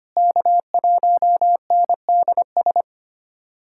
Morse code, heard as a keyed tone, about 25 wpm, 700 Hz. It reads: K1NDS
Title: USA Callsign DE Callsign 25wpm